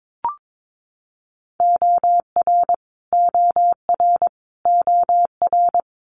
Morse code, heard as a keyed tone, about 22 wpm, 700 Hz. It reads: OR OR OR
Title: Top 200 Words Repeated 22wpm